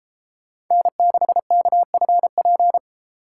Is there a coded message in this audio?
Yes: Morse code from 0.7 to 2.8 s, heard as a keyed tone, about 31 wpm, 700 Hz. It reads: N6KFP